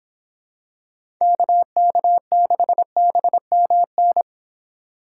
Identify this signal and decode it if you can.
Morse code, heard as a keyed tone, about 26 wpm, 700 Hz. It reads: KK6BMN